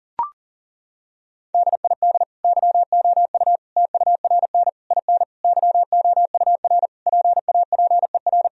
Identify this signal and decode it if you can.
Morse code, heard as a keyed tone, about 40 wpm, 700 Hz. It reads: DID YOU TURN IN YOUR PAPER
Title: Sentences from Top 1000 Words - Review Part 2 - 40wpm